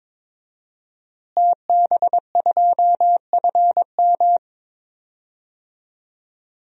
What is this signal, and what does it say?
Morse code, heard as a keyed tone, about 22 wpm, 700 Hz. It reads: TB2FM
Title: DXCC-100 Call Signs - 5 Characters - Repeated 2x - 22wpm